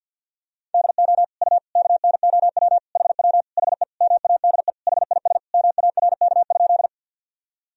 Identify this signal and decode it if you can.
Morse code, heard as a keyed tone, about 48 wpm, 700 Hz. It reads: DO U KNOW HW HE MADE HIS MARK?
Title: Sentences from Top 300 Words and Abbreviations 50wpm